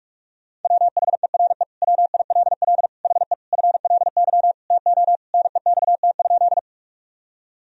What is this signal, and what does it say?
Morse code, heard as a keyed tone, about 46 wpm, 700 Hz. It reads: WHERE WILL HE FLY TO NEXT?
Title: Sentences from Top 1000 Words - Review Part 2 - 45wpm